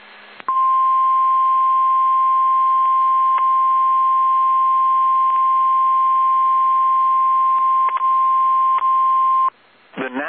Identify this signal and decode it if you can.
Emergency Alert System (SAME) tones (the 1050 Hz attention tone). It is heard from 0.5 to 9.5 s.
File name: eas / KSDZKDJL / KSDZKDJL